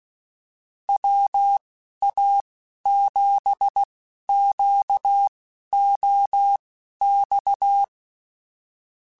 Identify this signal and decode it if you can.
Morse code, heard as a keyed tone, about 16 wpm, spaced as for 11 wpm, 800 Hz. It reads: WA7QOX